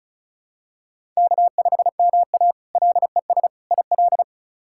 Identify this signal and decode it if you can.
Morse code, heard as a keyed tone, about 34 wpm, 700 Hz. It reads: K5MA LES IL
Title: K1USN SST QSO Practice - 35wpm